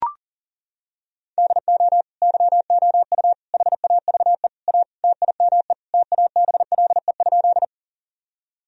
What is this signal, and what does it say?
Morse code, heard as a keyed tone, about 40 wpm, 700 Hz. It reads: DO YOU HAVE A TIME TABLE?
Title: Sentences from Top 1000 Words - Review Part 2 - 40wpm